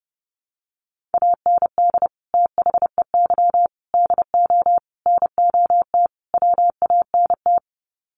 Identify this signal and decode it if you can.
Morse code, heard as a keyed tone, about 28 wpm, 700 Hz. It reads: AND THEY DO NOT WANT
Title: Sets of 5 Words Encore2 30wpm